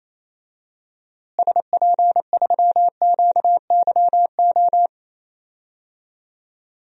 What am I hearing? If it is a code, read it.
Morse code, heard as a keyed tone, about 28 wpm, 700 Hz. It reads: SP3QYO
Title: DXCC-100 Call Signs - 6 Characters - Repeated 3x - 28wpm